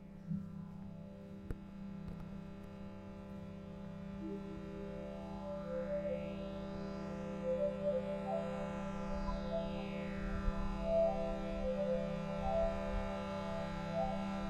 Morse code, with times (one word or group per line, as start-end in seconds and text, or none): none